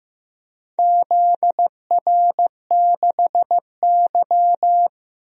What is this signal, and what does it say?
Morse code, heard as a keyed tone, about 15 wpm, 700 Hz. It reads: ZR6Y
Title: DXCC-100 Call Signs - 4 Characters - 15wpm